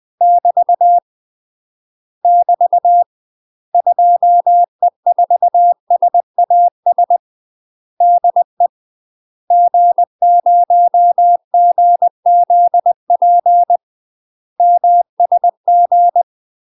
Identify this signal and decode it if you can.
Morse code, heard as a keyed tone, about 20 wpm, 700 Hz. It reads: <BT> <BT> 2E4SAS DE G0GZP MSG
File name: UK MASTER QSOs TRAINING / 20WPM / UK MASTER QSOs 0053